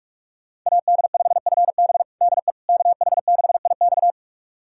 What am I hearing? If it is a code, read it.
Morse code, heard as a keyed tone, about 46 wpm, 700 Hz. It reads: AD5FB DE KH6IX
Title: USA Callsign DE Callsign - 5 Characters - 45wpm